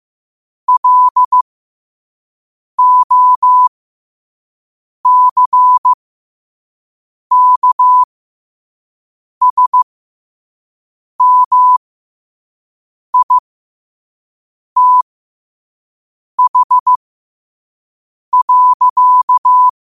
Morse code, heard as a keyed tone, about 15 wpm, spaced as for 5 wpm, 1000 Hz. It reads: LOCKSMITH.
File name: qotdfiles / 5wpm